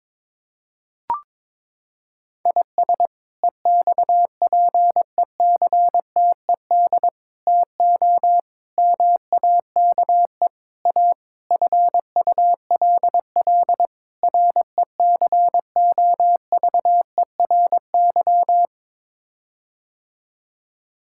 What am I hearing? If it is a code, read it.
Morse code, heard as a keyed tone, about 21 wpm, 700 Hz. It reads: IS EXPECTED TO MAKE A FULL RECOVERY
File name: practice / 22wpm / Sets of 7 Words - 22wpm